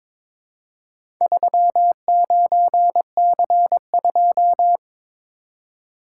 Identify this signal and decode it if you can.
Morse code, heard as a keyed tone, about 23 wpm, 700 Hz. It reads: 39C2
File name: practice / 22wpm / Four Letter-Number Combinations - 22wpm